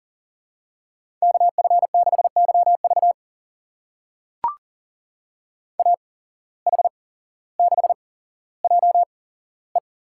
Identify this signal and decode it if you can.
Morse code, heard as a keyed tone, about 40 wpm, spaced as for 17 wpm, 700 Hz. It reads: KF6YV AH6JE
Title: USA Call Signs - 5 Characters - 8x Character Spacing - 40wpm